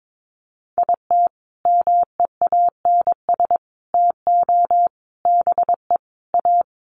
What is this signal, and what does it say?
Morse code, heard as a keyed tone, about 22 wpm, 700 Hz. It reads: IT MEANS TO BE A
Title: Sets of 5 Words Encore2 22wpm